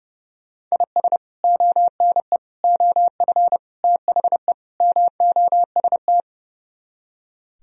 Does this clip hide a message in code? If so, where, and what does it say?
Yes: Morse code from 0.7 to 6.2 s, heard as a keyed tone, about 39 wpm, 700 Hz. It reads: IS ONE OF THE MOST